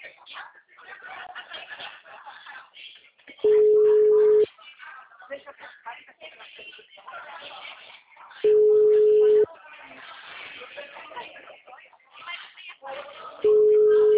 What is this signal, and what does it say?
Ringback tone (Europe, 425 Hz), with speech.